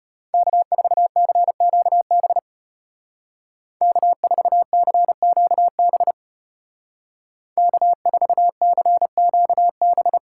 Morse code, heard as a keyed tone, about 32 wpm, 700 Hz. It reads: K4CQB K4CQB K4CQB